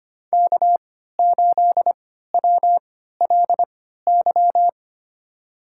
Morse code, heard as a keyed tone, about 25 wpm, spaced as for 14 wpm, 700 Hz. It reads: K8WLY
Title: USA Call Signs - 3x Character Spacing - 25wpm